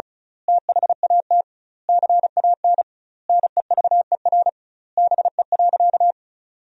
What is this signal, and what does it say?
Morse code, heard as a keyed tone, about 34 wpm, 700 Hz. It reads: THAT CAN NEVER BE.
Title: 2x Word-Spacing - Sentences from Top 300 Words - 35wpm